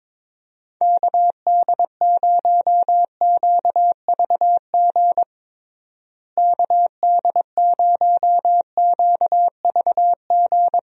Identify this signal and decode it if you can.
Morse code, heard as a keyed tone, about 22 wpm, 700 Hz. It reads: KD0QVG KD0QVG
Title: USA Call Signs - 6 Characters - Repeated 3x - 22wpm